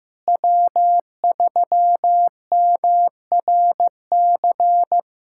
Morse code, heard as a keyed tone, about 15 wpm, 700 Hz. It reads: W3MRC
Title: USA Callsign DE Callsign - 5 Characters - 15wpm